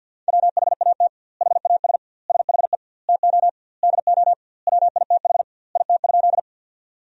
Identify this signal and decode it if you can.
Morse code, heard as a keyed tone, about 50 wpm, 700 Hz. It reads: WHAT HAS SHE TO DO WITH IT?
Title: 2x Word-Spacing - Sentences from Top 100 Words - 50wpm